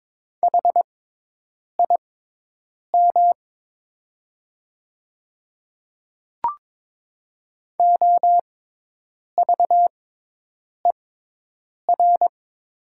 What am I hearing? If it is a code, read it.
Morse code, heard as a keyed tone, about 22 wpm, spaced as for 8 wpm, 700 Hz. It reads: HIM OVER
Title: 6x Character Spacing - Top 100 Words - 22wpm